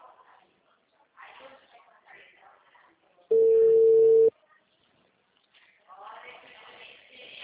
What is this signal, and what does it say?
Ringback tone (China, 450 Hz), with speech.
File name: out- -2012- - - .3014